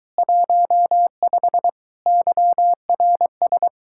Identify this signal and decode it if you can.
Morse code, heard as a keyed tone, about 23 wpm, 700 Hz. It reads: 15 YRS